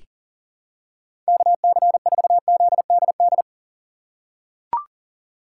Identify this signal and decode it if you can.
Morse code, heard as a keyed tone, about 40 wpm, 700 Hz. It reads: KC4ZDD
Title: USA Call Signs - 6 Characters - 6x Character Spacing - 40wpm